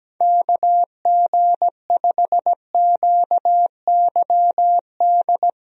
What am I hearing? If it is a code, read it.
Morse code, heard as a keyed tone, about 17 wpm, 700 Hz. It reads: KG5QYD
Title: USA Call Signs - 6 Characters - 17wpm